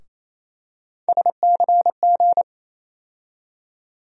Morse code, heard as a keyed tone, about 28 wpm, 700 Hz. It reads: SCG
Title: USA Call Sign Postfixes - 3 Letters 28wpm